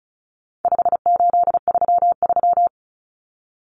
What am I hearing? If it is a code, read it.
Morse code, heard as a keyed tone, about 34 wpm, 700 Hz. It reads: 5833
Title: Numbers - 4-Digit 35wpm